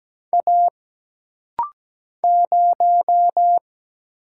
Morse code, heard as a keyed tone, about 17 wpm, 700 Hz. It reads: A 0
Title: Single Letter-Number Alt - Rapid-Fire 17wpm